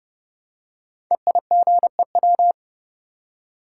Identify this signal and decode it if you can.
Morse code, heard as a keyed tone, about 31 wpm, 700 Hz. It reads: EIGEW
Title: Code Groups - 5 Letters - 30wpm